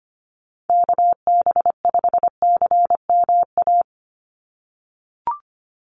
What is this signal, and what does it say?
Morse code, heard as a keyed tone, about 25 wpm, 700 Hz. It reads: KB5CMA
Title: USA Call Signs 25wpm